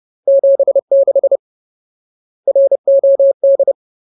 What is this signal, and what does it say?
Morse code, heard as a keyed tone, about 30 wpm, 550 Hz. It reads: 76 ROD
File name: Ultimate Word List / 30WPM Ultimate